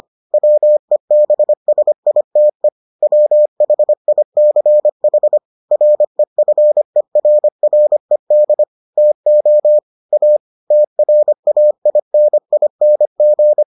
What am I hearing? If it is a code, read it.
Morse code, heard as a keyed tone, about 25 wpm, 600 Hz. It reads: WEBSITE WHICH REFERRED TO A TRAINING